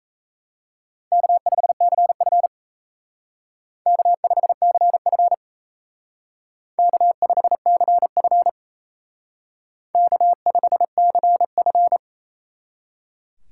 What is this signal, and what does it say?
Morse code, heard as a keyed tone, about 35 wpm, 700 Hz. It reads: K5CF K5CF K5CF K5CF